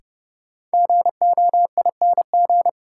Morse code, heard as a keyed tone, about 30 wpm, 700 Hz. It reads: GOING
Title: Sets of 2 Words - New Word Drill - 30wpm